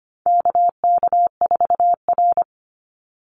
Morse code, heard as a keyed tone, about 25 wpm, 700 Hz. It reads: KK4R
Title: Call Signs - 25wpm